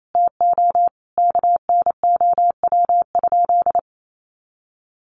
Morse code, heard as a keyed tone, about 28 wpm, 700 Hz. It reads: TO KNOW?
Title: Sentences from Top 200 Words 28wpm